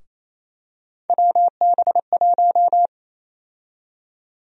Morse code, heard as a keyed tone, about 28 wpm, 700 Hz. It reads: WB1
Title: USA Call Sign Prefixes - Repeated 2x - 28wpm